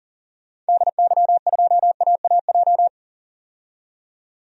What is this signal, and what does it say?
Morse code, heard as a keyed tone, about 40 wpm, 700 Hz. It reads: DY2AAJ
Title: DXCC-100 Call Signs - 6 Characters - 40wpm